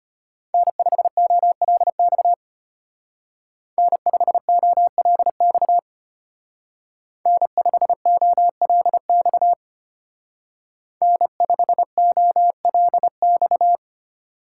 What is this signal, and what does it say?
Morse code, heard as a keyed tone, about 32 wpm, 700 Hz. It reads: N5OLX N5OLX N5OLX N5OLX